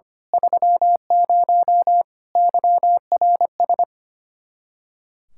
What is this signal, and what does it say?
Morse code, heard as a keyed tone, about 25 wpm, 700 Hz. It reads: 30 YRS
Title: Speed-Racing Common QSO Phrases - 25wpm